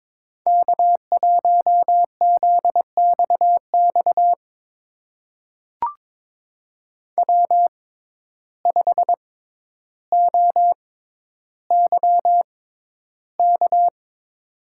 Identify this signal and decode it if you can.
Morse code, heard as a keyed tone, about 22 wpm, spaced as for 11 wpm, 700 Hz. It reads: K1ZXX W5OYK
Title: USA Call Signs - 6x Character Spacing - 22wpm